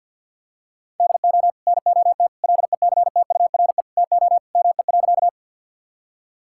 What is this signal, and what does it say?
Morse code, heard as a keyed tone, about 50 wpm, 700 Hz. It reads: DO NOT LECTURE TO ME.
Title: Sentences from Top 1600 Words 50wpm